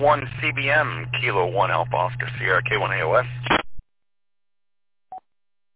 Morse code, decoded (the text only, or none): none